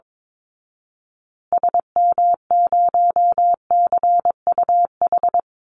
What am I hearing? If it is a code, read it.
Morse code, heard as a keyed tone, about 22 wpm, 700 Hz. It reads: SM0CUH
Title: Call Signs Repeated 2x - 22wpm